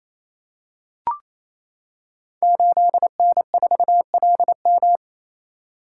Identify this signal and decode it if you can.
Morse code, heard as a keyed tone, about 27 wpm, 700 Hz. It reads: 8N4LM
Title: Code Groups - 5 Letter:Numbers - 28wpm